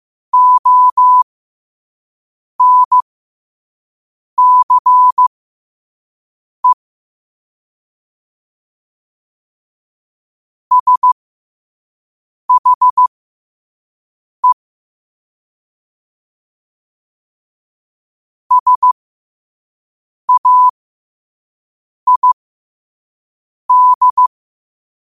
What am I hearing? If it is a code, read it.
Morse code, heard as a keyed tone, about 15 wpm, spaced as for 5 wpm, 1000 Hz. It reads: ONCE SHE SAID
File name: qotdfiles / 5wpm